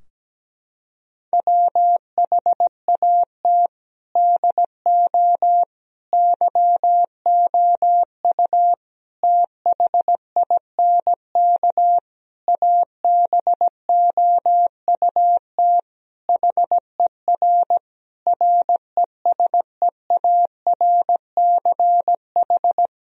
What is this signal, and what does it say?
Morse code, heard as a keyed tone, about 17 wpm, 700 Hz. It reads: WHAT DO YOU THINK ABOUT HER RESEARCH
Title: Sentences from Top 1000 Words - Review Part 4 - 17wpm